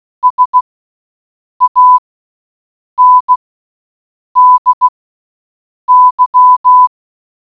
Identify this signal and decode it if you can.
Morse code, heard as a keyed tone, about 16 wpm, spaced as for 7 wpm, 1000 Hz. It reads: SANDY